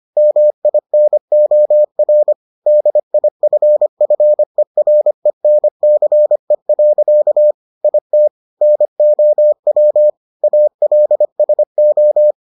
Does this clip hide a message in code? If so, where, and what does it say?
Yes: Morse code from 0.2 to 12.3 s, heard as a keyed tone, about 25 wpm, 600 Hz. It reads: MINOR DIFFERENCE. IT NOW ALSO